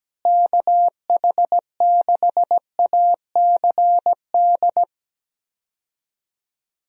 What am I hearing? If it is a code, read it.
Morse code, heard as a keyed tone, about 17 wpm, 700 Hz. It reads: KH6ACD
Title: USA Call Signs - 6 Characters - Repeated 3x - 17wpm